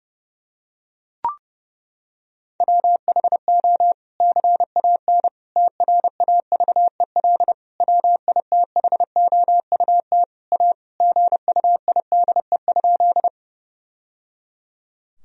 Morse code, heard as a keyed tone, about 30 wpm, 700 Hz. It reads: WHO CAN TRAVEL WITHOUT A GUIDE?
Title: Sentences from Top 1000 Words - Review Part 3 - 30wpm